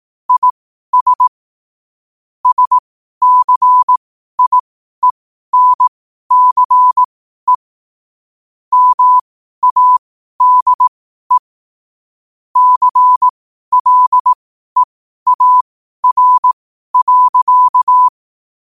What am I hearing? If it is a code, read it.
Morse code, heard as a keyed tone, about 18 wpm, spaced as for 12 wpm, 1000 Hz. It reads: IS SCIENCE MADE CLEAR.
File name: qotdfiles / 12wpm